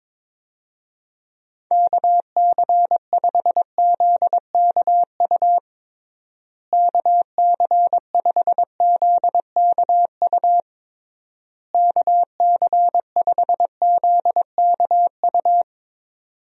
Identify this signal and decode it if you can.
Morse code, heard as a keyed tone, about 22 wpm, 700 Hz. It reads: KC5ZKU KC5ZKU KC5ZKU
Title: USA Call Signs - 6 Characters - Repeated 3x - 22wpm